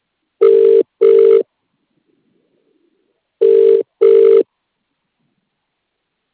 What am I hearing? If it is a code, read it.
Ringback tone (UK), 400 + 450 Hz.